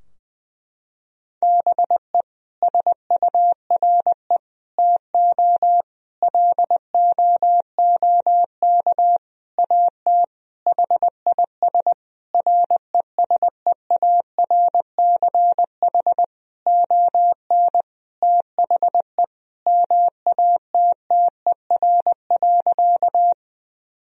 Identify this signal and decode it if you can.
Morse code, heard as a keyed tone, about 20 wpm, 700 Hz. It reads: BE SURE TO LOOK AT HIS RESEARCH ON THE MATTER.
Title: Sentences from Top 1000 Words - Review Part 4 - 20wpm